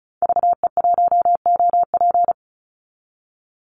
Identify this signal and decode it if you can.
Morse code, heard as a keyed tone, about 34 wpm, 700 Hz. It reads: VE1OP
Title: Call Signs Repeated 2x - 35wpm